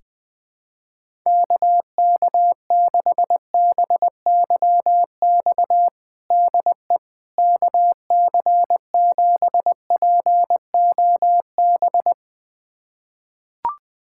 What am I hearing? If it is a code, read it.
Morse code, heard as a keyed tone, about 21 wpm, 700 Hz. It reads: KK6BYX DE KC7POB
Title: USA Callsign DE Callsign - 6 Characters - 20wpm